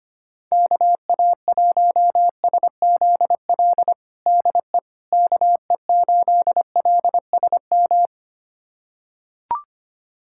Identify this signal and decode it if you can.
Morse code, heard as a keyed tone, about 25 wpm, 700 Hz. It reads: KA1SZL DE KE8LSM
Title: USA Callsign DE Callsign - 6 Characters - 25wpm